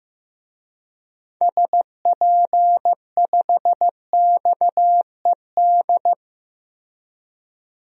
Morse code, heard as a keyed tone, about 15 wpm, 700 Hz. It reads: SP5XED
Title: DXCC-100 Call Signs - 6 Characters - 15wpm